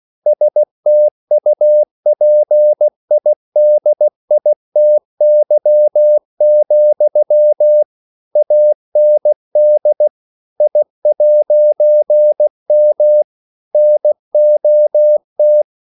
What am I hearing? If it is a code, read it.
Morse code, heard as a keyed tone, about 16 wpm, 600 Hz. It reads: STUPIDITY, AND I'M NOT
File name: CW / Practice / EnglishQuotations0116